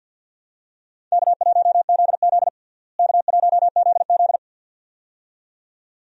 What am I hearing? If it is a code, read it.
Morse code, heard as a keyed tone, about 49 wpm, 700 Hz. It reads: K1ZZ K1ZZ
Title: USA Call Signs - 4 Characters - Repeated 2x - 50wpm